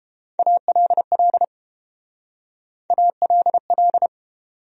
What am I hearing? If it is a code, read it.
Morse code, heard as a keyed tone, about 32 wpm, 700 Hz. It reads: ALL ALL